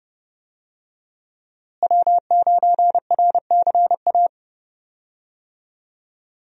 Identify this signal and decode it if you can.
Morse code, heard as a keyed tone, about 30 wpm, 700 Hz. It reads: W9RCA
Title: USA Call Signs - 5 Characters - Repeated 2x - 30wpm